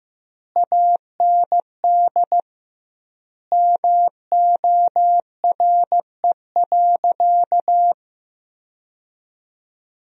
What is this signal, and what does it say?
Morse code, heard as a keyed tone, about 15 wpm, 700 Hz. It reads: AND MORE.
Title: 2x Word-Spacing - Sentences from Top 500 Words - 15wpm